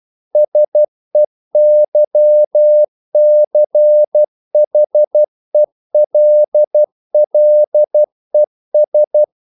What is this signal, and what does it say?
Morse code, heard as a keyed tone, about 12 wpm, 600 Hz. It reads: SEYCHELLES
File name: CW / Practice / NationalCapitals12